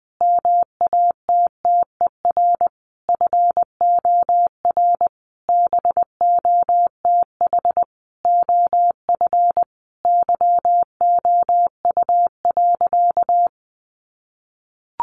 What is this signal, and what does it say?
Morse code, heard as a keyed tone, about 19 wpm, 700 Hz. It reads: MATTER FOR BOTH OF YOU.
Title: Sentences from Top 1100 Words 20wpm